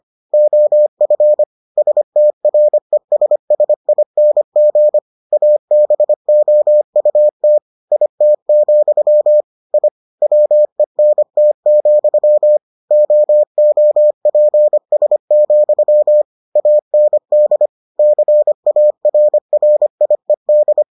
Morse code, heard as a keyed tone, about 25 wpm, 600 Hz. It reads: OF STRESSING ABOUT IT, I WENT, OOPS, AND CARRIED